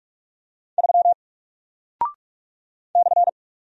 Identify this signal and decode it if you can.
Morse code, heard as a keyed tone, about 44 wpm, 700 Hz. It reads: 3 /
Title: Single Character (Letter:Number:Punctuation:Prosign) Alt 45wpm